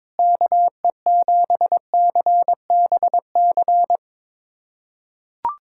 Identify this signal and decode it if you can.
Morse code, heard as a keyed tone, about 22 wpm, 700 Hz. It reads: KE7CBC
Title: USA Callsign DE Callsign 22wpm